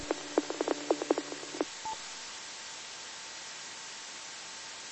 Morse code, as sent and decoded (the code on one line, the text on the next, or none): none